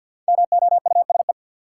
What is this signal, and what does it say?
Morse code, heard as a keyed tone, about 53 wpm, 700 Hz. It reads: MOUSE
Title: Five Letter Words - 50wpm